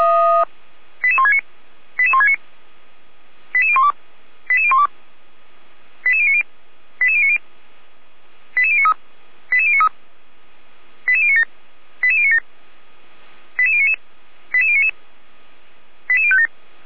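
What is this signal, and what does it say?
Five-tone selective call (ZVEI), heard as quick runs of five tones, tones of about 70 ms. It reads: 88169 88169 88012 88012 88090 88090 88093 88093 88097 88097 88099 88099 88057
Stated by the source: ZVEI five-tone call 88103 received with an SDR, archived on the Feuerwehr Orlamünde audio server.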